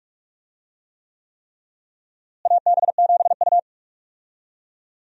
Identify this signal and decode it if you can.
Morse code, heard as a keyed tone, about 46 wpm, 700 Hz. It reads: AB7U
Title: USA Call Signs - 4 Characters - Repeated 3x - 45wpm